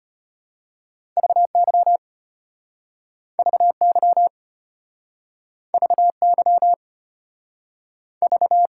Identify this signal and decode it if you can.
Morse code, heard as a keyed tone, about 33 wpm, 700 Hz. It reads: VY VY VY V